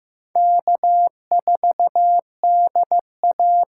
Morse code, heard as a keyed tone, about 15 wpm, 700 Hz. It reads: K4DA
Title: USA Call Signs - 5 Characters - 6x Character Spacing - 15wpm